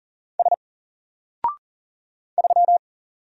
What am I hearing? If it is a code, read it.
Morse code, heard as a keyed tone, about 40 wpm, 700 Hz. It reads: S 3
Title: Single Character (Letter:Number:Punctuation:Prosign) Alt 40wpm